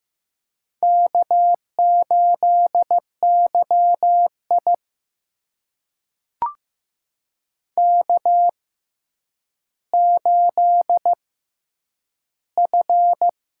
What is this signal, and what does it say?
Morse code, heard as a keyed tone, about 15 wpm, spaced as for 9 wpm, 700 Hz. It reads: K8YI K8F
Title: USA Call Signs - 4 Characters - 6x Character Spacing - 15wpm